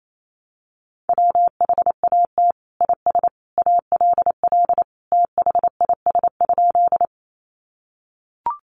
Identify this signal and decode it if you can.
Morse code, heard as a keyed tone, about 28 wpm, 700 Hz. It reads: WHAT IS ALL THIS?
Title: Sentences from Top 100 Words 28wpm